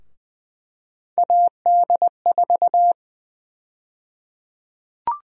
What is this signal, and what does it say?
Morse code, heard as a keyed tone, about 20 wpm, 700 Hz. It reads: AD4
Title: USA Call Sign Prefixes - Repeated 2x - 20wpm